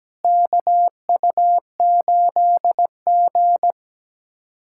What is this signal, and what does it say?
Morse code, heard as a keyed tone, about 17 wpm, 700 Hz. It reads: KU8G
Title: USA Callsign DE Callsign - 4 Characters - 17wpm